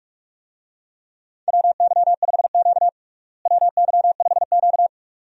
Morse code, heard as a keyed tone, about 46 wpm, 700 Hz. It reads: WY5Q WY5Q
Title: USA Call Signs - 4 Characters - Repeated 2x - 45wpm